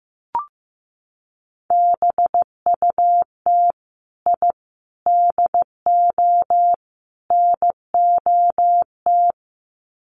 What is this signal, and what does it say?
Morse code, heard as a keyed tone, about 15 wpm, 700 Hz. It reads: BUT I DO NOT
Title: Sets of 4 Words 15wpm